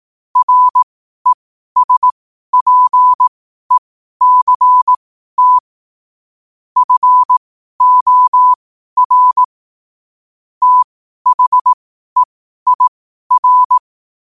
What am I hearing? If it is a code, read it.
Morse code, heard as a keyed tone, about 18 wpm, spaced as for 12 wpm, 1000 Hz. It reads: RESPECT FOR THEIR
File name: qotdfiles / 12wpm